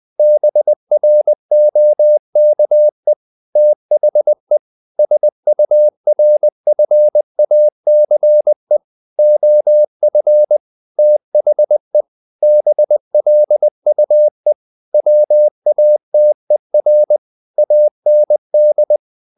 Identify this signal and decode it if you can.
Morse code, heard as a keyed tone, about 20 wpm, 600 Hz. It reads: BROKE THE SURFACE OF THE BLUE WATER AND
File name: the old man and the sea part 2 20wpm